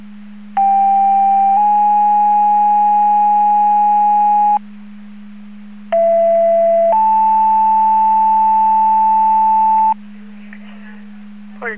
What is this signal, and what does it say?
Two-tone page, heard as a one-second tone, then a three-second tone, about 790 then 830 Hz, about 670 then 880 Hz, with speech. It is heard from 0.6 to 9.9 s.